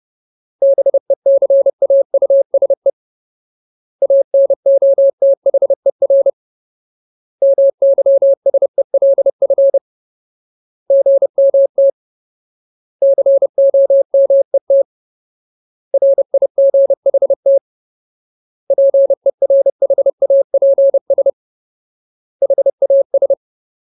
Morse code, heard as a keyed tone, about 30 wpm, 550 Hz. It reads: BECAUSE ANOTHER MYSELF GMT COMET RIGHT PERHAPS HAS
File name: Ultimate Word List / 30WPM Ultimate